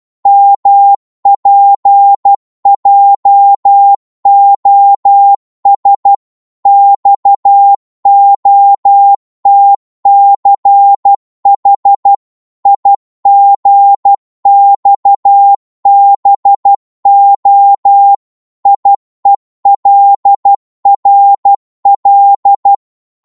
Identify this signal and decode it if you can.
Morse code, heard as a keyed tone, about 12 wpm, 800 Hz. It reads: MPJOS XOTCH IGXBO IELRL
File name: Morse / ltr-12wpm